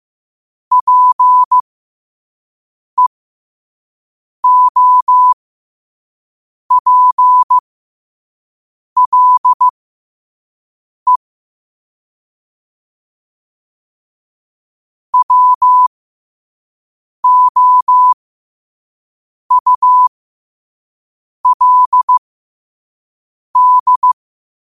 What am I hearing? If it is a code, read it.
Morse code, heard as a keyed tone, about 15 wpm, spaced as for 5 wpm, 1000 Hz. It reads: PEOPLE WOULD
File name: qotdfiles / 5wpm